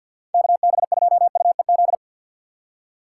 Morse code, heard as a keyed tone, about 53 wpm, 700 Hz. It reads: KB2UEZ
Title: USA Call Signs - 6 Characters - 3x Character Spacing - 50wpm